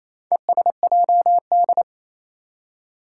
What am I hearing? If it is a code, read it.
Morse code, heard as a keyed tone, about 27 wpm, 700 Hz. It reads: ESJD
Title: Four Challenging Letter-Number Combinations 28wpm